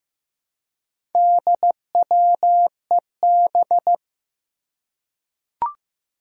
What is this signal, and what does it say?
Morse code, heard as a keyed tone, about 15 wpm, 700 Hz. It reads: DWEB
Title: Four Challenging Letter-Number Combinations 15wpm